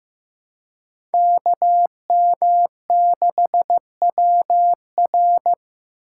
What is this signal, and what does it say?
Morse code, heard as a keyed tone, about 15 wpm, 700 Hz. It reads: KM6WR
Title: USA Call Signs - 5 Characters - 4x Character Spacing - 15wpm